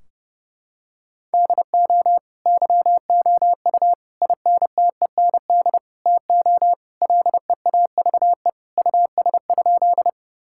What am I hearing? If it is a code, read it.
Morse code, heard as a keyed tone, about 30 wpm, 700 Hz. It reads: DO YOU INTEND TO LEAVE US?
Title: Sentences from Top 1600 Words 30wpm